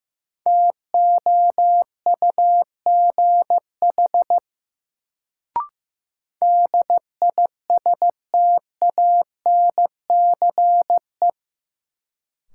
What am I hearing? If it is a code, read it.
Morse code, heard as a keyed tone, about 15 wpm, 700 Hz. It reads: TOUGH DISTANCE
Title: Sentences from Top 1200 Words - New Word Drill Alt - 15wpm